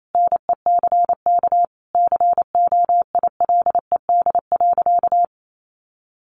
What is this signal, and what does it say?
Morse code, heard as a keyed tone, about 28 wpm, 700 Hz. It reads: NECK COILED.
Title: A Sound Of Thunder 28wpm